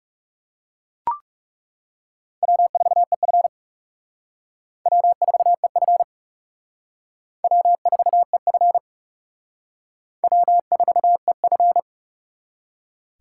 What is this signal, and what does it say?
Morse code, heard as a keyed tone, about 37 wpm, 700 Hz. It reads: W4EF W4EF W4EF W4EF
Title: Speed Racing - USA 1x2 Call Signs 30wpm